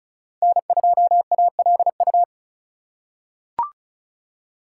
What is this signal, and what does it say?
Morse code, heard as a keyed tone, about 34 wpm, 700 Hz. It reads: N2ALU
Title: USA Call Signs - 5 Characters - 4x Character Spacing - 35wpm